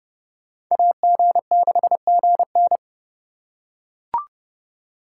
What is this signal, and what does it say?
Morse code, heard as a keyed tone, about 29 wpm, 700 Hz. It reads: AG6GN